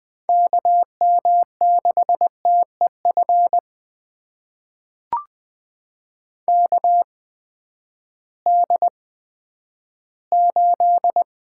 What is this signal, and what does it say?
Morse code, heard as a keyed tone, about 20 wpm, spaced as for 11 wpm, 700 Hz. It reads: KM6TEF KD8
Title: USA Call Signs - 6 Characters - 8x Character Spacing - 20wpm